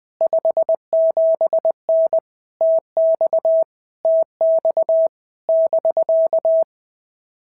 Morse code, heard as a keyed tone, about 20 wpm, 650 Hz. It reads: 57N TX TX <BK>
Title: POTA QSO Practice - 20wpm